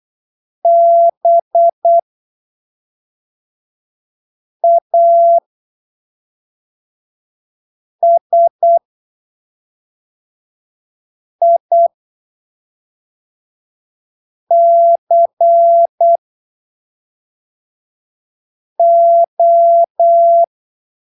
Morse code, heard as a keyed tone, about 8 wpm, spaced as for 3 wpm, 680 Hz. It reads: BASICO
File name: arquivos / texto / texto 09 08ppm